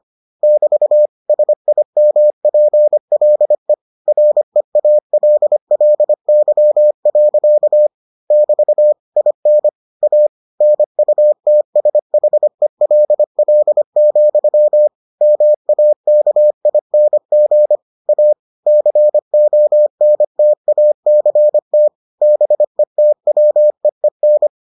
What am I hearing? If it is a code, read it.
Morse code, heard as a keyed tone, about 25 wpm, 600 Hz. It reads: <BT> SIMPLE REALLY. <BT> IN A NUTSHELL, MAKING A CONTACT BETWEEN